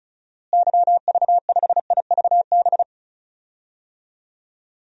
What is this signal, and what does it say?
Morse code, heard as a keyed tone, about 36 wpm, 700 Hz. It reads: YV5IVB
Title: DXCC-100 Call Signs - 6 Characters - Repeated 3x - 35wpm